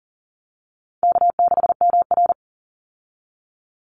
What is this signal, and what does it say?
Morse code, heard as a keyed tone, about 40 wpm, 700 Hz. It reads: K6MR
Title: Easy Call Signs Repeated 3x - 40wpm